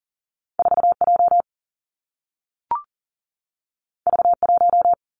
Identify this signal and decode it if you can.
Morse code, heard as a keyed tone, about 43 wpm, 700 Hz. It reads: 4 J V 1
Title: Two Challenging Letter-Number Combinations 40wpm